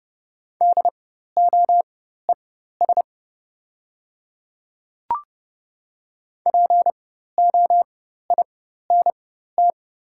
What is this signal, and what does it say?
Morse code, heard as a keyed tone, about 29 wpm, spaced as for 14 wpm, 690 Hz. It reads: DOES POINT
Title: 4x Character Spacing - Top 200 Words - 30wpm